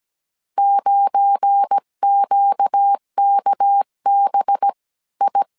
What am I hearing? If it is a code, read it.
Morse code, heard as a keyed tone, about 17 wpm, spaced as for 14 wpm, 800 Hz. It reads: 9QKBI
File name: CW / DET / 1 9 1500C S B 16wpm 2856